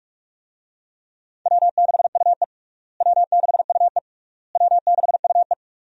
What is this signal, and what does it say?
Morse code, heard as a keyed tone, about 44 wpm, 700 Hz. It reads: W6UE W6UE W6UE
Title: USA Call Signs - 4 Characters - Repeated 3x - 45wpm